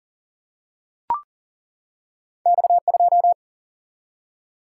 Morse code, heard as a keyed tone, about 38 wpm, 700 Hz. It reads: X2
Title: Single-Letter Single-Number 40wpm